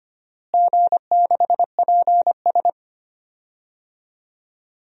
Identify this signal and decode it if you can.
Morse code, heard as a keyed tone, about 25 wpm, 700 Hz. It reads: G6PS
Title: DXCC-100 Call Signs - 4 Characters - Repeated 2x - 25wpm